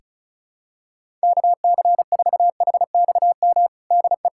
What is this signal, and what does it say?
Morse code, heard as a keyed tone, about 34 wpm, 700 Hz. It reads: KC4HXM DE
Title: USA Callsign DE Callsign 35wpm